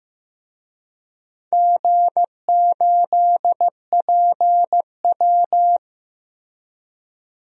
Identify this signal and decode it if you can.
Morse code, heard as a keyed tone, about 15 wpm, 700 Hz. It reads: G8PW
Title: DXCC-100 Call Signs - 4 Characters - Repeated 2x - 15wpm